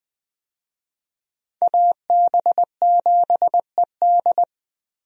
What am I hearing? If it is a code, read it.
Morse code, heard as a keyed tone, about 20 wpm, 700 Hz. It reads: AB7ED
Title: USA Call Signs - 5 Characters - Repeated 3x - 20wpm